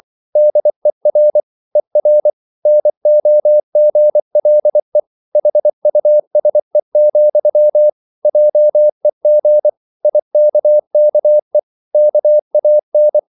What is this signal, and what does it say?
Morse code, heard as a keyed tone, about 24 wpm, 600 Hz. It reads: DER ER NOGLE HUSE, JEG IKKE KAN
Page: Ferie 24 wpm